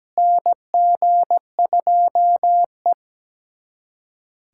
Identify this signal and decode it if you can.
Morse code, heard as a keyed tone, about 17 wpm, 700 Hz. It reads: NG2E
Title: USA Callsign DE Callsign - 4 Characters - 17wpm